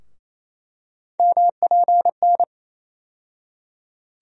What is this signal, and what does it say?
Morse code, heard as a keyed tone, about 28 wpm, 700 Hz. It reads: MPN
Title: USA Call Sign Postfixes - 3 Letters 28wpm